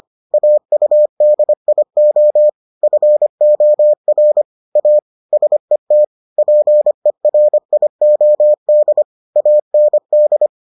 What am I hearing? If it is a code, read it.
Morse code, heard as a keyed tone, about 25 wpm, 600 Hz. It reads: AUDIO FOR A SET PERIOD AND